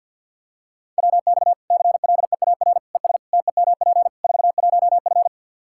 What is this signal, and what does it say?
Morse code, heard as a keyed tone, about 48 wpm, 700 Hz. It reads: WX CLEAR ES TEMP 41F
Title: K1USN SST QSO Practice 50wpm